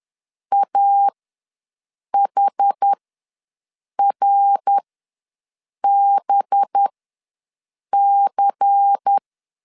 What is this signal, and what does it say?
Morse code, heard as a keyed tone, about 10 wpm, spaced as for 6 wpm, 800 Hz. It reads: AHRBC
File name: LEZIONI / 6 PKX0 / 6 PKX ws 10wpm 3706